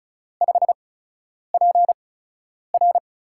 Morse code, heard as a keyed tone, about 34 wpm, spaced as for 10 wpm, 700 Hz. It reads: 5PR
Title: USA Call Signs - 5 Characters - 8x Character Spacing - 35wpm